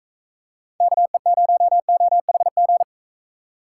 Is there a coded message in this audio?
Yes: Morse code from 0.8 to 2.8 s, heard as a keyed tone, about 45 wpm, 700 Hz. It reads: KE0OHG